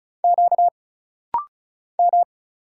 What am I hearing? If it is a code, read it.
Morse code, heard as a keyed tone, about 34 wpm, 700 Hz. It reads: Q M
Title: Single Letters - Rapid-Fire 35wpm